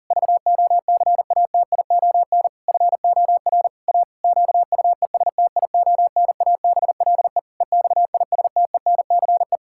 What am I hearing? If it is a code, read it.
Morse code, heard as a keyed tone, about 40 wpm, 700 Hz. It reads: VOCATION FOR A QUESTIONABLE EXISTENCE
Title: Pangram Challenge v2 - 40wpm